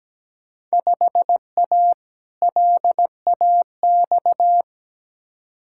Morse code, heard as a keyed tone, about 17 wpm, 700 Hz. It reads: 5A LAX
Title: Field Day Exchanges - 17wpm